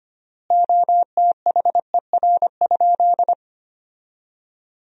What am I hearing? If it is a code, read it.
Morse code, heard as a keyed tone, about 25 wpm, 700 Hz. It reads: OTHER?
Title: 3x Word-Spacing - Sentences from Top 200 Words - 25wpm